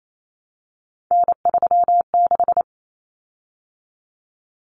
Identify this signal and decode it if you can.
Morse code, heard as a keyed tone, about 28 wpm, 700 Hz. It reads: N36
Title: Common Cut Numbers - 3-Digit 28wpm